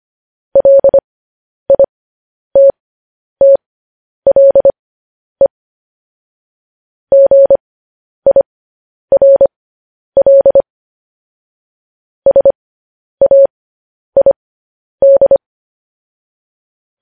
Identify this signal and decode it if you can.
Morse code, heard as a keyed tone, about 25 wpm, spaced as for 10 wpm, 550 Hz. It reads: LITTLE GIRL SAID